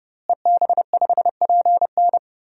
Morse code, heard as a keyed tone, about 30 wpm, 700 Hz. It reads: EB5PN
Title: DXCC-100 Call Signs - 5 Characters - Repeated 2x - 30wpm